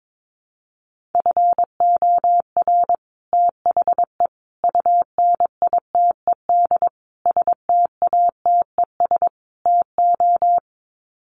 Morse code, heard as a keyed tone, about 22 wpm, 700 Hz. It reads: FOR THE UNITED STATES TO
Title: Sets of 5 Words Encore 22wpm